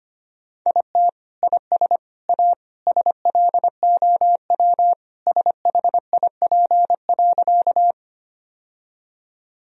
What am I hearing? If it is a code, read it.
Morse code, heard as a keyed tone, about 25 wpm, 700 Hz. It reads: IT IS A SLOW SHIP.
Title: Sentences from Top 1000 Words - Review Part 2 - 25wpm